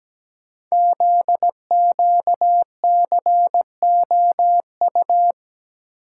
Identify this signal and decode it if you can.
Morse code, heard as a keyed tone, about 17 wpm, 700 Hz. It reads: ZQCOU
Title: Code Groups - 5 Letters - 17wpm